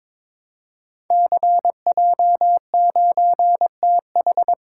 Morse code, heard as a keyed tone, about 23 wpm, 700 Hz. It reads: CJ9TH
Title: Code Groups - 5 Letter:Numbers - 22wpm